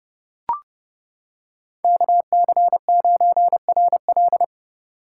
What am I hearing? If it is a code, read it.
Morse code, heard as a keyed tone, about 28 wpm, 700 Hz. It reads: KC9RL